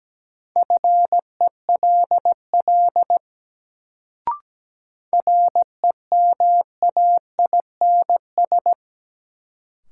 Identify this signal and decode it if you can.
Morse code, heard as a keyed tone, about 17 wpm, 700 Hz. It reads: FELL REMAINS
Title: Sets of 5 Words Encore - New Word Drill - 17wpm